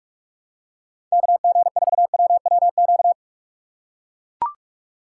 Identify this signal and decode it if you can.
Morse code, heard as a keyed tone, about 42 wpm, 700 Hz. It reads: KG4WWQ
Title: USA Call Signs - 6 Characters - 4x Character Spacing - 45wpm